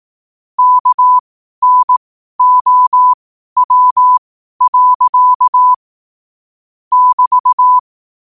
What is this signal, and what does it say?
Morse code, heard as a keyed tone, about 18 wpm, spaced as for 12 wpm, 1000 Hz. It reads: KNOW. <BT>
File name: 12wpm